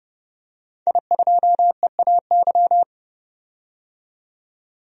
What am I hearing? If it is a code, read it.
Morse code, heard as a keyed tone, about 30 wpm, 700 Hz. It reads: I2EAY
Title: DXCC-100 Call Signs - 5 Characters - 30wpm